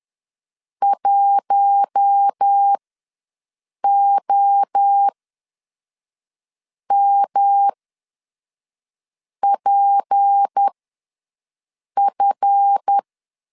Morse code, heard as a keyed tone, about 10 wpm, spaced as for 5 wpm, 800 Hz. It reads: 1OMPF
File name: LEZIONI / 7 GF9Z / 7 GF ws 10wpm 3918